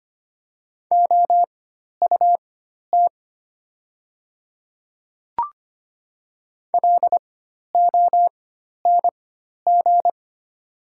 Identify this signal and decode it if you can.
Morse code, heard as a keyed tone, about 25 wpm, spaced as for 12 wpm, 700 Hz. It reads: OUT LONG
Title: 4x Character Spacing - Top 100 Words - 25wpm